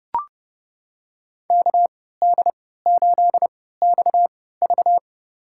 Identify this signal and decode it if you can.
Morse code, heard as a keyed tone, about 31 wpm, spaced as for 17 wpm, 700 Hz. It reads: KD8XV
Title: USA Call Signs - 5 Characters - 3x Character Spacing - 30wpm